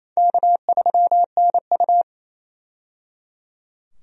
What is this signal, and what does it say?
Morse code, heard as a keyed tone, about 28 wpm, 700 Hz. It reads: K3NU
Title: USA Call Signs - 4 Characters - 28wpm